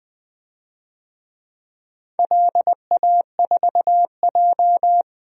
Morse code, heard as a keyed tone, about 20 wpm, 700 Hz. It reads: LA4J
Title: DXCC-100 Call Signs - 4 Characters - 20wpm